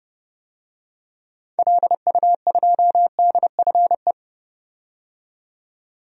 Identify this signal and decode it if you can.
Morse code, heard as a keyed tone, about 30 wpm, 700 Hz. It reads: LU2DFE
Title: DXCC-100 Call Signs - 6 Characters - 30wpm